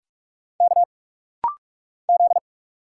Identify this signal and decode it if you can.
Morse code, heard as a keyed tone, about 46 wpm, 700 Hz. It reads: K Z
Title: Single Letter-Number Alt - Rapid-Fire 45wpm